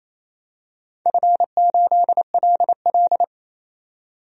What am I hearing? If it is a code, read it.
Morse code, heard as a keyed tone, about 29 wpm, 700 Hz. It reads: F8LL
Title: DXCC-100 Call Signs - 4 Characters - 28wpm